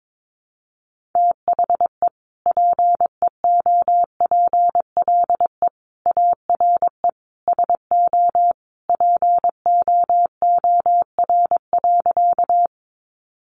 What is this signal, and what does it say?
Morse code, heard as a keyed tone, about 22 wpm, 700 Hz. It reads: THE PEOPLE ARE SO POOR.
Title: Sentences from Top 1000 Words 22wpm